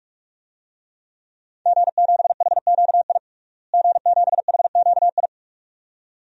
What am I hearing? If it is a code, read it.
Morse code, heard as a keyed tone, about 42 wpm, 700 Hz. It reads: G7HQI G7HQI
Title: DXCC-100 Call Signs - 5 Characters - Repeated 2x - 45wpm